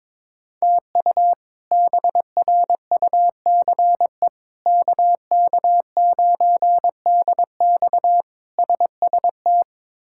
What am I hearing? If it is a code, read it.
Morse code, heard as a keyed tone, about 22 wpm, 700 Hz. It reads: TU BRUCE KK9DX SST
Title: K1USN SST QSO Practice - 22wpm